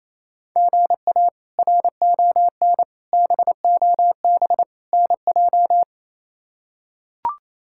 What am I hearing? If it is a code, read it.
Morse code, heard as a keyed tone, about 28 wpm, 700 Hz. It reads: GA RON BOB NJ
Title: K1USN SST QSO Practice - 28wpm